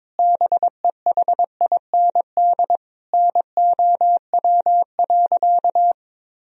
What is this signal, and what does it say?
Morse code, heard as a keyed tone, about 22 wpm, 700 Hz. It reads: BEHIND NOW.
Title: Sentences from Top 1000 Words - Review Part 1 - 22wpm